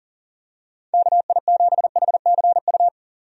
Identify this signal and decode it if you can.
Morse code, heard as a keyed tone, about 40 wpm, 700 Hz. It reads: KI7HCU
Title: USA Call Signs - 6 Characters - 40wpm